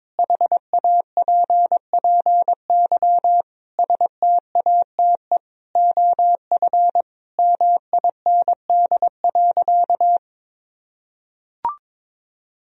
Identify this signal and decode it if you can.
Morse code, heard as a keyed tone, about 22 wpm, 700 Hz. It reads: HAPPY STATE OF MIND.
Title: Sentences from Top 1000 Words - Review Part 2 - 22wpm